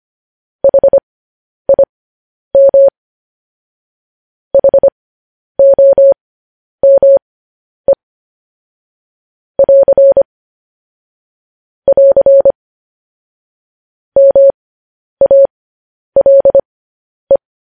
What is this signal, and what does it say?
Morse code, heard as a keyed tone, about 25 wpm, spaced as for 10 wpm, 550 Hz. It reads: HIM HOME <AR> <AR> MALE